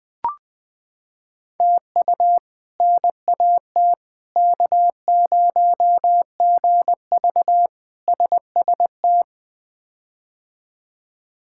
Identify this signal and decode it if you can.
Morse code, heard as a keyed tone, about 20 wpm, 700 Hz. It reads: TU NAT K0GV SST
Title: K1USN SST QSO Practice - 20wpm